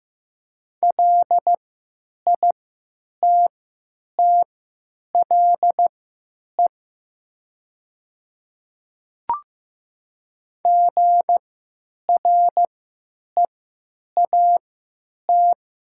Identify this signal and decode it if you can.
Morse code, heard as a keyed tone, about 15 wpm, spaced as for 9 wpm, 700 Hz. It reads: LITTLE GREAT
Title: 3x Character Spacing - Top 200 Words - 15wpm